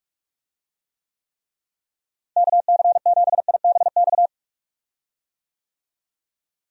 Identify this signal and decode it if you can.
Morse code, heard as a keyed tone, about 41 wpm, 700 Hz. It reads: KC7IBX
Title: USA Call Signs - 6 Characters - Repeated 3x - 45wpm